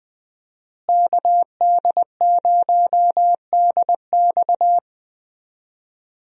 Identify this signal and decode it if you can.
Morse code, heard as a keyed tone, about 20 wpm, 700 Hz. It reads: KD0DX
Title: USA Call Signs - 5 Characters - Repeated 2x - 20wpm